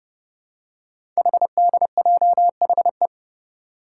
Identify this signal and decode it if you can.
Morse code, heard as a keyed tone, about 28 wpm, 700 Hz. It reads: HDJHE
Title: Code Groups - 5 Letters - 30wpm